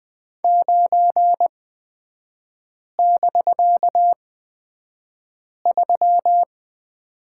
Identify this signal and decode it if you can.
Morse code, heard as a keyed tone, about 20 wpm, spaced as for 5 wpm, 700 Hz. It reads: 9<BK>3
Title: 250 Lesson 40e - BK - Three Characters 20wpm